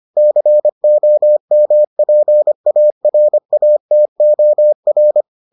Morse code, heard as a keyed tone, about 25 wpm, 600 Hz. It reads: COMPARATOR